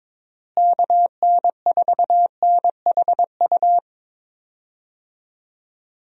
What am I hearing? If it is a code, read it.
Morse code, heard as a keyed tone, about 22 wpm, 700 Hz. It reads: KN4NHU
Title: USA Call Signs - 6 Characters - Repeated 2x - 22wpm